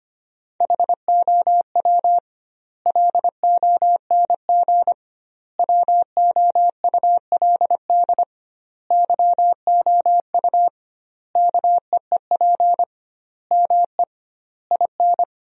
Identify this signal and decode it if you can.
Morse code, heard as a keyed tone, about 25 wpm, 700 Hz. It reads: HOW LONG WOULD YOU KEEP ME IN
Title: 2x Word-Spacing - Sentences from Top 500 Words - 25wpm